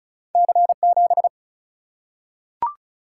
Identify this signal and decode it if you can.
Morse code, heard as a keyed tone, about 34 wpm, 700 Hz. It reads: C7
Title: Single-Letter Single-Number 35wpm